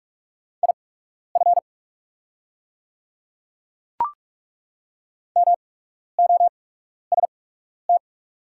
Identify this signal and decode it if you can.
Morse code, heard as a keyed tone, about 43 wpm, spaced as for 12 wpm, 700 Hz. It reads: IF MOST
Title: 8x Character Spacing - Top 100 Words - 45wpm